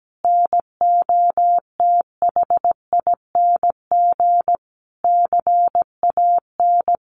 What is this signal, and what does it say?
Morse code, heard as a keyed tone, about 17 wpm, 700 Hz. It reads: NOTHING CAN
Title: Sentences from Top 1100 Words 17wpm